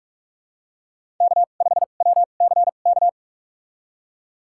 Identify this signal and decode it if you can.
Morse code, heard as a keyed tone, about 48 wpm, spaced as for 33 wpm, 700 Hz. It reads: K5WCK
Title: USA Call Signs - 5 Characters - 2x Character Spacing - 45wpm